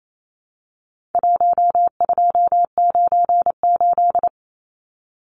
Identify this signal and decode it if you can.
Morse code, heard as a keyed tone, about 28 wpm, 700 Hz. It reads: 1298
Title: Numbers - 4-Digit 28wpm